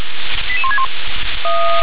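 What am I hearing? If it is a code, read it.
Five-tone selective call (ZVEI), heard as quick runs of five tones, tones of about 70 ms. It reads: 88161
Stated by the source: ZVEI five-tone selective call received by SDR.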